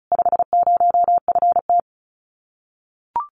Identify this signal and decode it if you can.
Morse code, heard as a keyed tone, about 34 wpm, 700 Hz. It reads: 50FT
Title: Common QSO Phrases Encore 35wpm